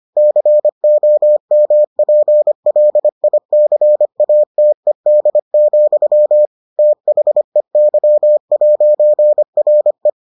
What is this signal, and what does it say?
Morse code, heard as a keyed tone, about 25 wpm, 600 Hz. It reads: COMPLICATED, THEY'RE